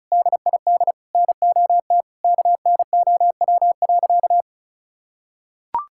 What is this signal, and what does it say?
Morse code, heard as a keyed tone, about 34 wpm, 700 Hz. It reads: DID NOT KNOW.
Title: Sentences from Top 1600 Words 35wpm